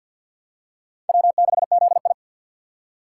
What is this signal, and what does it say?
Morse code, heard as a keyed tone, about 53 wpm, 700 Hz. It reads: W6ZI
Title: USA Call Signs - 4 Characters - 3x Character Spacing - 50wpm